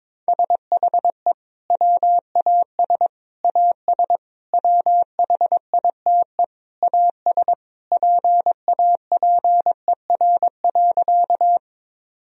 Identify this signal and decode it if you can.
Morse code, heard as a keyed tone, about 22 wpm, 700 Hz. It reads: SHE WAS AS WHITE AS PAPER.
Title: Sentences from Top 1000 Words - Review Part 1 - 22wpm